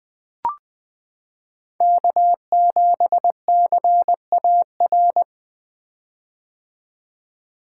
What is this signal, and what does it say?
Morse code, heard as a keyed tone, about 20 wpm, 700 Hz. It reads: K7CAR
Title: 2023 POTA Activator Call Signs - 20wpm